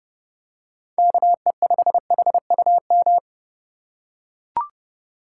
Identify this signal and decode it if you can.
Morse code, heard as a keyed tone, about 30 wpm, 700 Hz. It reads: KE5HUM